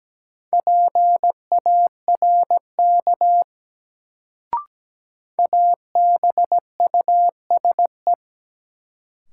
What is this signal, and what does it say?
Morse code, heard as a keyed tone, about 17 wpm, 700 Hz. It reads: PARK ABUSE
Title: Sentences from Top 1300 Words - New Word Drill - 17wpm